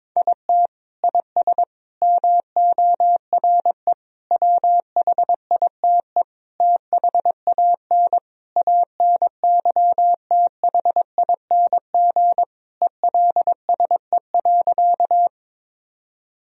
Morse code, heard as a keyed tone, about 22 wpm, 700 Hz. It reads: IT IS MORE WHITE THAN ANYTHING ELSE.
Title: Sentences from Top 1000 Words - Review Part 1 - 22wpm